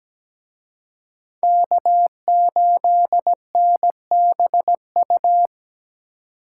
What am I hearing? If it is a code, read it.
Morse code, heard as a keyed tone, about 20 wpm, 700 Hz. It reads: K8NBU
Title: Speed Racing - USA 1x3 Call Signs 17wpm